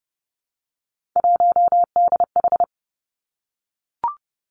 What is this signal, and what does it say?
Morse code, heard as a keyed tone, about 30 wpm, 700 Hz. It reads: 1DH
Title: Three Challenging Letter-Number Combinations 30wpm